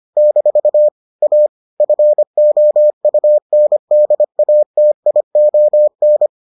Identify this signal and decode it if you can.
Morse code, heard as a keyed tone, about 25 wpm, 600 Hz. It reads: - A FOUNDATION